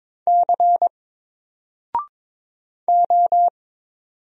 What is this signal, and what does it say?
Morse code, heard as a keyed tone, about 22 wpm, 700 Hz. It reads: C O
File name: Single-Letter-Number-22wpm